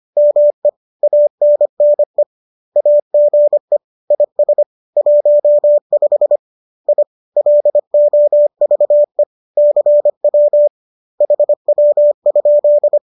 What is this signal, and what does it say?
Morse code, heard as a keyed tone, about 25 wpm, 590 Hz. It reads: ME ANNE AGE IS 15 I LOVE CW HW?